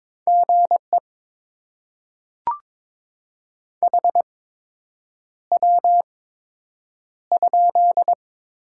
Morse code, heard as a keyed tone, about 22 wpm, spaced as for 8 wpm, 700 Hz. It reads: GE HW?
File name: Top 20 QSO Elements - 8x Character Spacing - 22wpm